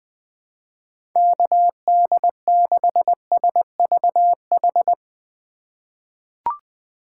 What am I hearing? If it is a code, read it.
Morse code, heard as a keyed tone, about 20 wpm, 700 Hz. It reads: KD6SVH
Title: USA Call Signs - 6 Characters - 2x Character Spacing - 20wpm